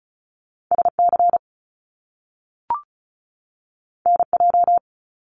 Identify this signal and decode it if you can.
Morse code, heard as a keyed tone, about 36 wpm, 700 Hz. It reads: SC NJ
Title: US States 35wpm